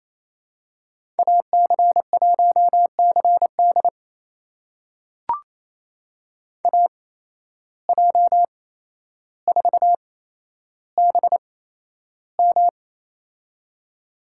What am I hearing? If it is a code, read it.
Morse code, heard as a keyed tone, about 28 wpm, spaced as for 12 wpm, 700 Hz. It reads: AC1CD AJ4BM
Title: USA Call Signs - 8x Character Spacing - 28wpm